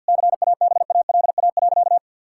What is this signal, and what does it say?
Morse code, heard as a keyed tone, about 50 wpm, 700 Hz. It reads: CABALA.
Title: Pangram Challenge v2 - 50wpm